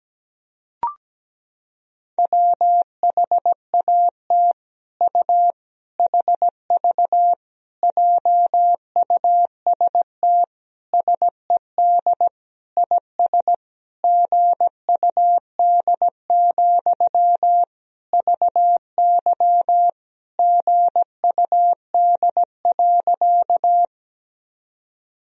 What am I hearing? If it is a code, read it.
Morse code, heard as a keyed tone, about 17 wpm, 700 Hz. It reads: WHAT U HV JUST SED IS GUD, VY GUD.
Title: Sentences from Top 200 Words and Abbreviations 17wpm